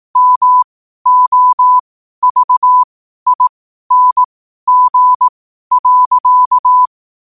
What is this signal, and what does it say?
Morse code, heard as a keyed tone, about 18 wpm, spaced as for 12 wpm, 1000 Hz. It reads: MOVING.
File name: qotdfiles / 12wpm